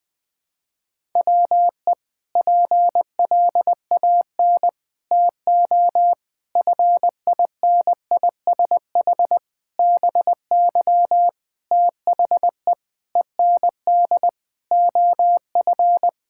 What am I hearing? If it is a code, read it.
Morse code, heard as a keyed tone, about 20 wpm, 700 Hz. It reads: WE PLAN TO FINISH BY THE END OF
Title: Sentences with 5-Word Phrases - 20wpm